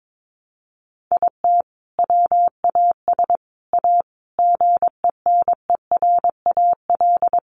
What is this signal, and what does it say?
Morse code, heard as a keyed tone, about 22 wpm, 700 Hz. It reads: IT WAS A GENERAL
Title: Sentences from Top 600 Words 22wpm